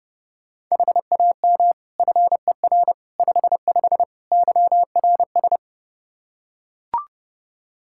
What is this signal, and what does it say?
Morse code, heard as a keyed tone, about 30 wpm, 700 Hz. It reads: HAM FER 55 YRS
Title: Practice QSO - Exchange 3 - More Info - 30wpm